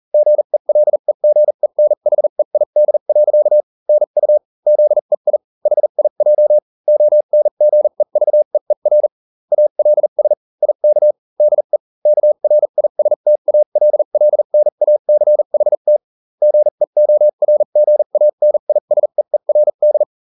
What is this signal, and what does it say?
Morse code, heard as a keyed tone, about 40 wpm, spaced as for 35 wpm, 600 Hz. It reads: GELEGENHEID. NU ZEI HIJ ONGEVEER ALS IK DE KRISTALLNACHT GEORGANISEERD